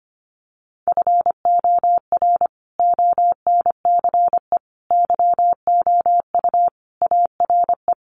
Morse code, heard as a keyed tone, about 25 wpm, 700 Hz. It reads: FOR ONCE YOU ARE
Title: Sentences from Top 300 Words 25wpm